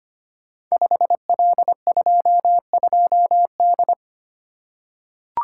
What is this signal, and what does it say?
Morse code, heard as a keyed tone, about 25 wpm, 690 Hz. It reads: 5L22D
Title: Code Groups - 5 Letter:Numbers - 25wpm